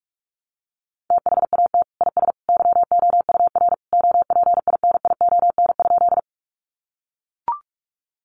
Morse code, heard as a keyed tone, about 44 wpm, 700 Hz. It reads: THAT IS YOUR OPINION?
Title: Sentences from Top 1200 Words 45wpm